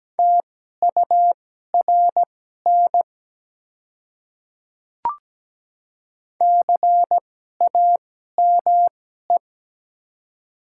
Morse code, heard as a keyed tone, about 17 wpm, spaced as for 12 wpm, 700 Hz. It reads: TURN CAME
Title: 2x Character Spacing - Top 300 Words - 17wpm